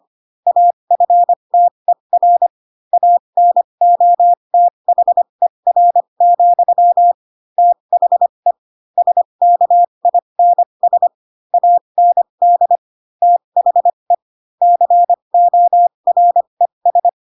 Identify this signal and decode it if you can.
Morse code, heard as a keyed tone, about 25 wpm, spaced as for 22 wpm, 700 Hz. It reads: AFTER ANOTHER, THE SKINS AND THE CORES
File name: Pinocchio / Pinocchio ch7 18 25 ik1hge